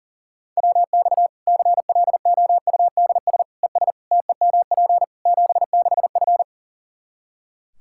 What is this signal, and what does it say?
Morse code, heard as a keyed tone, about 40 wpm, 700 Hz. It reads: WX CLOUDS ES TEMP 76F
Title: K1USN SST QSO Practice 40wpm